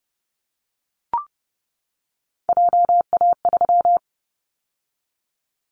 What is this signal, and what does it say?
Morse code, heard as a keyed tone, about 30 wpm, 700 Hz. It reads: JA3
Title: Callsign Prefixes for Top 25 DXCC - 30wpm